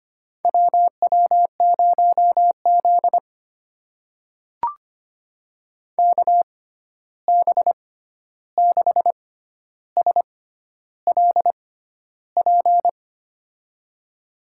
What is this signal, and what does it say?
Morse code, heard as a keyed tone, about 25 wpm, spaced as for 11 wpm, 700 Hz. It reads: WW0Z KB6SLP
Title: USA Call Signs - 6x Character Spacing - 25wpm